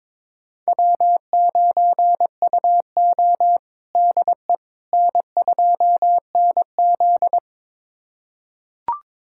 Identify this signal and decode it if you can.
Morse code, heard as a keyed tone, about 22 wpm, 700 Hz. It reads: W9UO DE N2NZ
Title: USA Callsign DE Callsign - 4 Characters - 22wpm